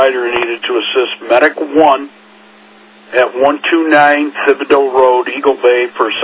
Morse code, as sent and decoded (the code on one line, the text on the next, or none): none